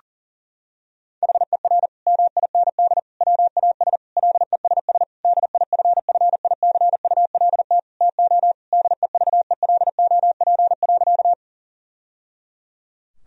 Morse code, heard as a keyed tone, about 40 wpm, 700 Hz. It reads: HER MIND WAS LESS DIFFICULT TO DEVELOP.
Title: Sentences from Top 1000 Words - Review Part 2 - 40wpm